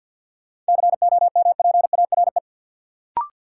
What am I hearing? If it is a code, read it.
Morse code, heard as a keyed tone, about 52 wpm, 700 Hz. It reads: COMPARE
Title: Sentences from Top 1300 Words - New Word Drill - 50wpm